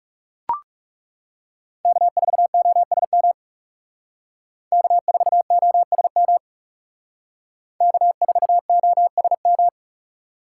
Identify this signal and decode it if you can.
Morse code, heard as a keyed tone, about 39 wpm, 700 Hz. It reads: K4OSM K4OSM K4OSM